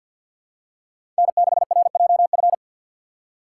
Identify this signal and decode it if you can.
Morse code, heard as a keyed tone, about 47 wpm, 700 Hz. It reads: N6RJF
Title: USA Call Signs - 5 Characters - 3x Character Spacing - 50wpm